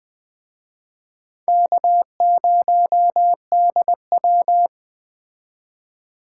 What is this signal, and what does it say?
Morse code, heard as a keyed tone, about 20 wpm, 700 Hz. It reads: K0DW
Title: Speed Racing - USA 1x2 Call Signs 17wpm